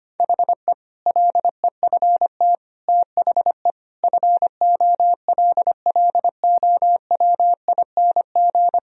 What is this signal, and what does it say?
Morse code, heard as a keyed tone, about 25 wpm, 700 Hz. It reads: HE LEFT THE FOLLOWING
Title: Sentences from Top 1000 Words - Review Part 4 - 25wpm